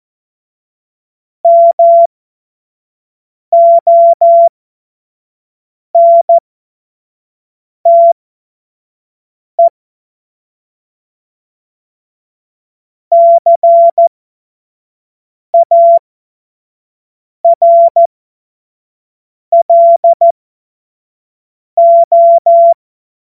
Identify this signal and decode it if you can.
Morse code, heard as a keyed tone, about 14 wpm, spaced as for 5 wpm, 680 Hz. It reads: MONTE CARLO
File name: Morsekurs / Witze / Witz 04. Oktober 2022 25 F70